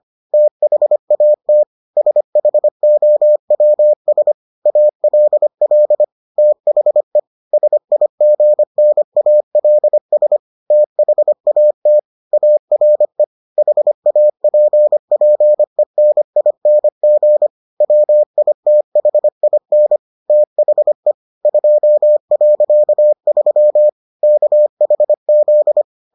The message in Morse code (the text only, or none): THAT SHOWS ALL THE SIGNALS THAT ARE HAPPENING WITHIN THE 2.3 KHZ